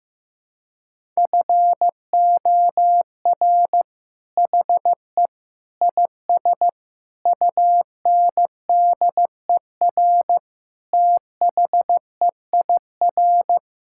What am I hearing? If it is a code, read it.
Morse code, heard as a keyed tone, about 16 wpm, 700 Hz. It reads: FOR HE IS UNDER THEIR
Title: Sentences from Top 1600 Words 15wpm